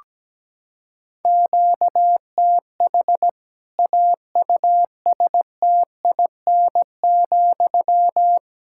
Morse code, heard as a keyed tone, about 17 wpm, 700 Hz. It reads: QTH AUSTIN,
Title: Practice QSO - Exchange 1 - Beginning - 17wpm